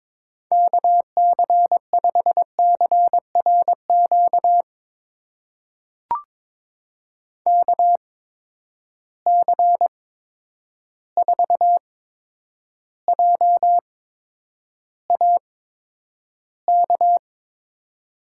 Morse code, heard as a keyed tone, about 22 wpm, spaced as for 9 wpm, 700 Hz. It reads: KC5CRQ KC4JAK
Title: USA Call Signs - 6 Characters - 8x Character Spacing - 22wpm